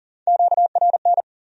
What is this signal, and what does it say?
Morse code, heard as a keyed tone, about 54 wpm, spaced as for 47 wpm, 700 Hz. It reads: QRN
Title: Speed-Racing Q-Codes - 40wpm